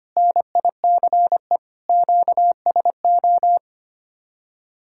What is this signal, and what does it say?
Morse code, heard as a keyed tone, about 25 wpm, 700 Hz. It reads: NICE QSO
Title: Speed-Racing Common QSO Phrases - 25wpm